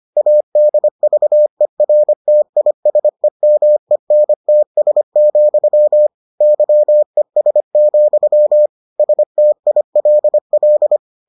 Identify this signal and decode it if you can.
Morse code, heard as a keyed tone, about 25 wpm, 600 Hz. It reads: ADVERTISEMENTS, YES, STILL